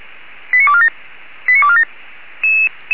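Five-tone selective call (ZVEI), heard as quick runs of five tones, tones of about 70 ms. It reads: 89247 89247
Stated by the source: Received on an SDR, ZVEI five-tone selective call, channel 464.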